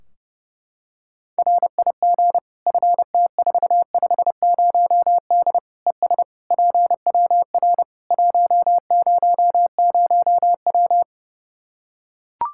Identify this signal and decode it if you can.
Morse code, heard as a keyed tone, about 32 wpm, 700 Hz. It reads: RIG FT450D ES PWR 100W
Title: Practice QSO - Exchange 2 - Info - 30wpm